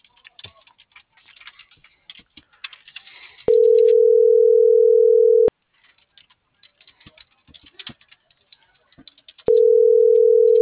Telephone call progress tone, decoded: ringback tone

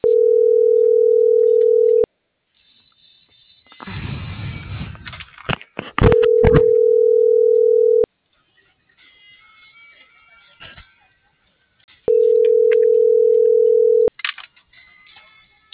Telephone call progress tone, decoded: ringback tone

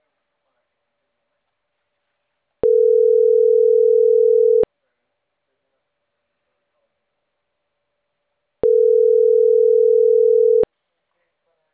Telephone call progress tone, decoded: ringback tone

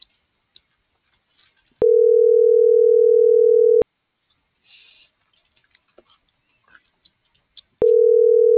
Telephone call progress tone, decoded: ringback tone

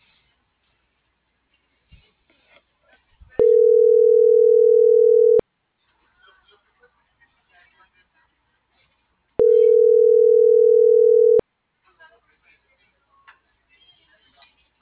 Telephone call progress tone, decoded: ringback tone

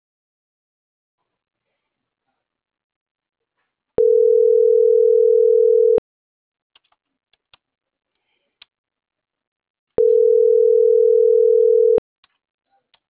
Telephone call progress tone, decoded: ringback tone

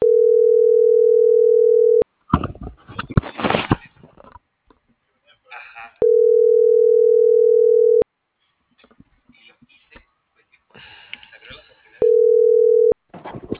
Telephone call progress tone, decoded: ringback tone

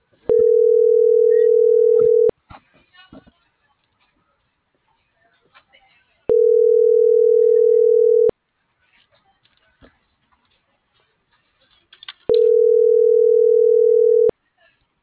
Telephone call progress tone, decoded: ringback tone

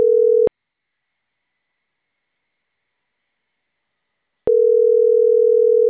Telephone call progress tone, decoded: ringback tone

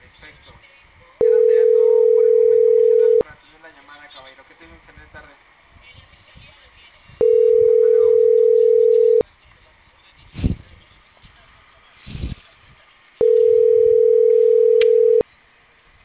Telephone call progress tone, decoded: ringback tone

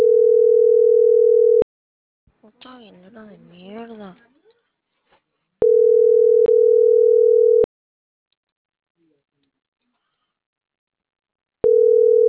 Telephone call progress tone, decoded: ringback tone

